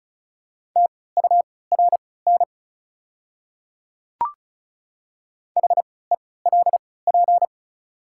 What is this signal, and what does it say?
Morse code, heard as a keyed tone, about 34 wpm, spaced as for 20 wpm, 700 Hz. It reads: TURN HELP
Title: 3x Character Spacing - Top 200 Words - 35wpm